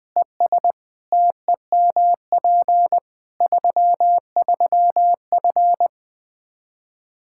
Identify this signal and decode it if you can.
Morse code, heard as a keyed tone, about 21 wpm, 700 Hz. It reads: ES TEMP 33F
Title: Practice QSO - Exchange 2 - Info - 20wpm